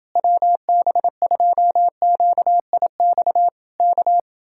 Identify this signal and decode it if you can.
Morse code, heard as a keyed tone, about 28 wpm, 700 Hz. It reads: WB2QIX K